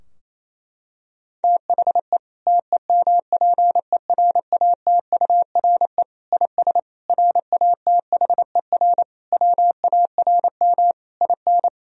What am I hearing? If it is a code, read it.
Morse code, heard as a keyed tone, about 28 wpm, 700 Hz. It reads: THE TEMPERATURE IS RATHER WARM IN
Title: Sentences from Top 1000 Words - Review Part 4 - 28wpm